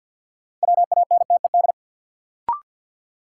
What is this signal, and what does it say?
Morse code, heard as a keyed tone, about 50 wpm, 700 Hz. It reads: WANTED
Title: Sets of 2 Words - New Word Drill - 50wpm